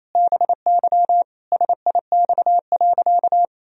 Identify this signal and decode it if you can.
Morse code, heard as a keyed tone, about 28 wpm, 700 Hz. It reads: BY SIX.
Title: Sentences from Top 1000 Words - Review Part 2 - 28wpm